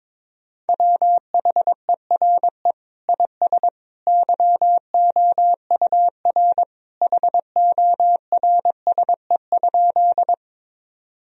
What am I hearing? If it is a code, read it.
Morse code, heard as a keyed tone, about 22 wpm, 700 Hz. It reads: WHERE IS YOUR HORSE?
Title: Sentences from Top 1000 Words - Review Part 1 - 22wpm